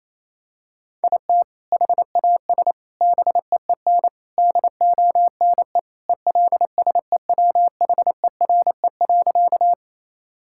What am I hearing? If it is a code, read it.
Morse code, heard as a keyed tone, about 28 wpm, 700 Hz. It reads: IT HAS BEEN DONE ELSEWHERE.
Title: Sentences from Top 1600 Words 28wpm